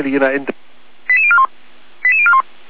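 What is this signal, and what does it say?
Five-tone selective call (ZVEI), heard as quick runs of five tones, tones of about 70 ms, with speech. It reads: 88041 88041
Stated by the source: Five-tone ZVEI selective call received on an SDR.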